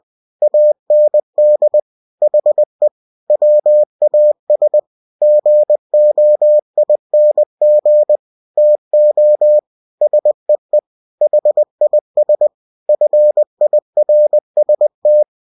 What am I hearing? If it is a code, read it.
Morse code, heard as a keyed tone, about 20 wpm, 600 Hz. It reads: AND HE WAS GOING TO SEE HIS FIRST
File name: cw-books / winnie-the-pooh / winnie-20wpm-page-0041